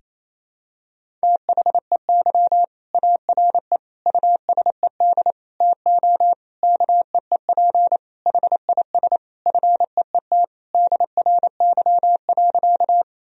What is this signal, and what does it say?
Morse code, heard as a keyed tone, about 28 wpm, 700 Hz. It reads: THEY ARE USED TO KEEP HIS FEET DRY.
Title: Sentences from Top 1000 Words - Review Part 3 - 28wpm